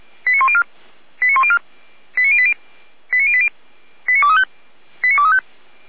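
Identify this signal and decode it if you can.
Five-tone selective call (ZVEI), heard as quick runs of five tones, tones of about 70 ms. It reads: 89104 89104 89080 89080 89235 89235
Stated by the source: ZVEI five-tone call received by SDR.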